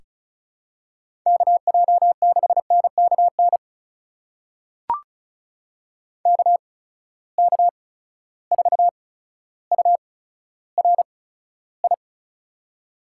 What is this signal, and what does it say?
Morse code, heard as a keyed tone, about 36 wpm, spaced as for 15 wpm, 700 Hz. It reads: KJ6NKN KK4URI
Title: USA Call Signs - 6 Characters - 8x Character Spacing - 35wpm